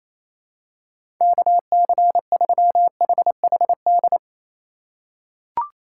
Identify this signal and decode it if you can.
Morse code, heard as a keyed tone, about 28 wpm, 700 Hz. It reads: KC3HHD
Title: USA Call Signs - 6 Characters - 8x Character Spacing - 28wpm